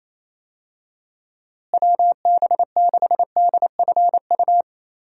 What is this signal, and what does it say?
Morse code, heard as a keyed tone, about 28 wpm, 700 Hz. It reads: WB6DFU
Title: USA Call Signs - 6 Characters - Repeated 2x - 28wpm